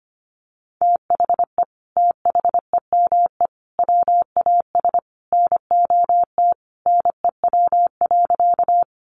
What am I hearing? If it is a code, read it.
Morse code, heard as a keyed tone, about 25 wpm, 700 Hz. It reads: THE THEME WAS NOT NEW.
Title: Sentences from Top 1100 Words 25wpm